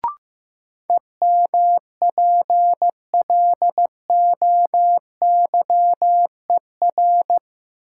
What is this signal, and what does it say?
Morse code, heard as a keyed tone, about 15 wpm, 700 Hz. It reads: EMPLOYER
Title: Sentences from Top 1600 Words - New Word Drill Alt - 15wpm